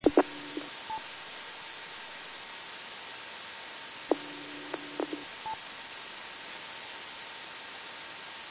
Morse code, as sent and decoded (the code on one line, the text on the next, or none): none